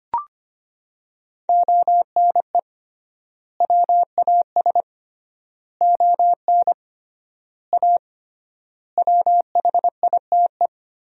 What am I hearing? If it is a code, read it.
Morse code, heard as a keyed tone, about 25 wpm, 700 Hz. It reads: ONE WAS ON A WHITE
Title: 3x Word-Spacing - Sentences from Top 300 Words - 25wpm